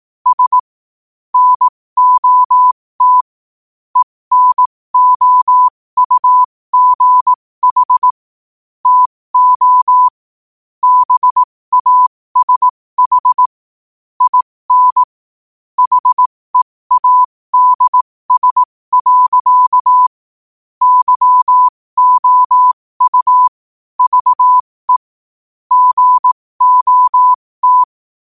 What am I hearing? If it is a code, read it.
Morse code, heard as a keyed tone, about 18 wpm, spaced as for 15 wpm, 1000 Hz. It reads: S NOT ENOUGH TO BASH IN HEADS. YOU VE GOT
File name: qotdfiles / 15wpm